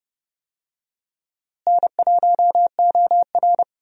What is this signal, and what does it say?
Morse code, heard as a keyed tone, about 30 wpm, 700 Hz. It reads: N1OR
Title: USA Call Signs - 4 Characters - Repeated 2x - 30wpm